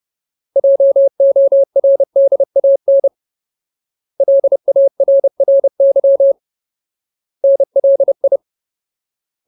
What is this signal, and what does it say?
Morse code, heard as a keyed tone, about 30 wpm, 550 Hz. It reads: JORDAN LARRY NLI
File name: Ultimate Word List / 30WPM Ultimate